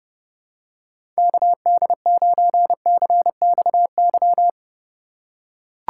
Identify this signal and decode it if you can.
Morse code, heard as a keyed tone, about 29 wpm, 700 Hz. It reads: KD9CXY
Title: USA Call Signs - 6 Characters - 6x Character Spacing - 30wpm